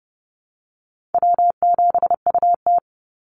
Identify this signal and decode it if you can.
Morse code, heard as a keyed tone, about 28 wpm, 700 Hz. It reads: W7UT
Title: Call Signs Repeated 2x - 30wpm